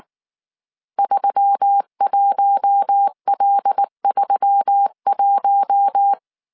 Morse code, heard as a keyed tone, about 19 wpm, 800 Hz. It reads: 31L31
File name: CW / DET / 1 6 1500C S C 18wpm 2356